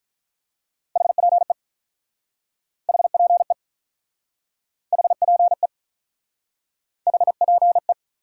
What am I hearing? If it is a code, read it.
Morse code, heard as a keyed tone, about 45 wpm, 700 Hz. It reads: HPE HPE HPE HPE